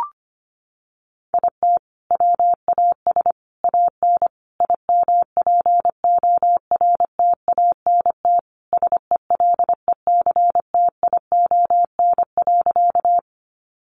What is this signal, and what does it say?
Morse code, heard as a keyed tone, about 25 wpm, 700 Hz. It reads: IT WAS AN IMPORTANT SELECTION.
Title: Sentences from Top 1400 Words 25wpm